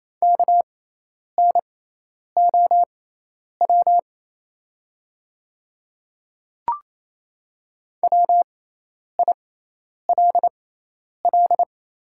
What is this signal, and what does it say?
Morse code, heard as a keyed tone, about 28 wpm, spaced as for 10 wpm, 700 Hz. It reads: KNOW WILL
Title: 6x Character Spacing - Top 100 Words - 28wpm